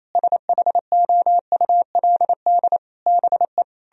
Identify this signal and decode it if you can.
Morse code, heard as a keyed tone, about 37 wpm, 700 Hz. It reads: SHOULD BE